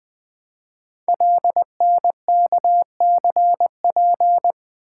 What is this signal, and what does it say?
Morse code, heard as a keyed tone, about 20 wpm, 700 Hz. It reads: LNKCP
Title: Code Groups - 5 Letter:Numbers - 20wpm